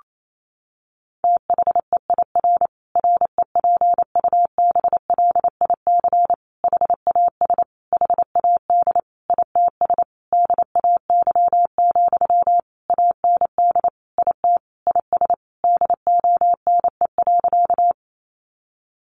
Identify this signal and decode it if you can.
Morse code, heard as a keyed tone, about 28 wpm, 700 Hz. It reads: THEIR REPUBLIC HAS HAD ITS DAY, AND IT IS DONE.
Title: Sentences from Top 1500 Words 28wpm